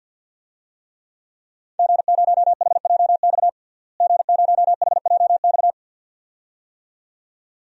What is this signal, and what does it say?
Morse code, heard as a keyed tone, about 48 wpm, 700 Hz. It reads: G0HJX G0HJX
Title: DXCC-100 Call Signs - 5 Characters - Repeated 2x - 50wpm